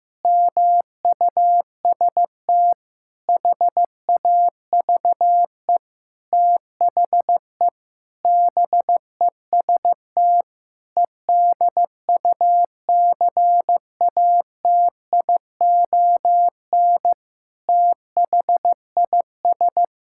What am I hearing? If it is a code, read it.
Morse code, heard as a keyed tone, about 15 wpm, 700 Hz. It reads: MUST HAVE THE BEST EDUCATION THIS
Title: Sentences from Top 1000 Words - Review Part 6 - 15wpm